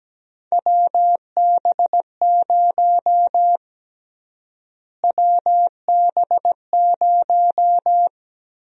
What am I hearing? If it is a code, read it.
Morse code, heard as a keyed tone, about 17 wpm, 700 Hz. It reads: WB0 WB0
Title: USA Call Sign Prefixes - Repeated 2x - 17wpm